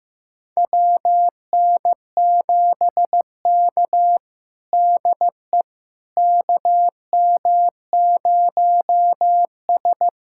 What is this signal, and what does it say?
Morse code, heard as a keyed tone, about 16 wpm, 700 Hz. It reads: WN7K DE KM0S
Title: USA Callsign DE Callsign - 4 Characters - 15wpm